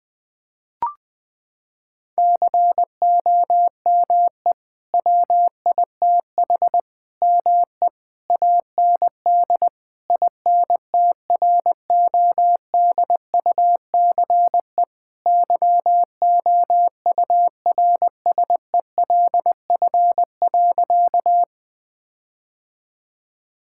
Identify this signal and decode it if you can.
Morse code, heard as a keyed tone, about 20 wpm, 700 Hz. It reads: COME WITH ME AND INTRODUCE YOURSELF.
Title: Sentences from Top 1600 Words 20wpm